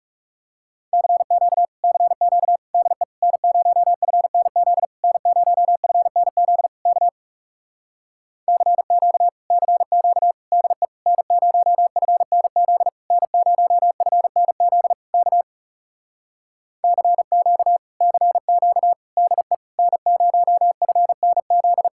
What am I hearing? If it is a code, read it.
Morse code, heard as a keyed tone, about 40 wpm, 700 Hz. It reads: CQ CQ DE N0FNZ N0FNZ K CQ CQ DE N0FNZ N0FNZ K CQ CQ DE N0FNZ